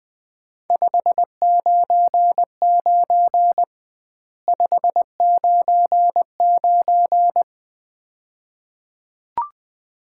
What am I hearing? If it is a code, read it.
Morse code, heard as a keyed tone, about 21 wpm, 700 Hz. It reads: 599 599
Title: Common QSO Phrases - 2x Word-Spacing - 20wpm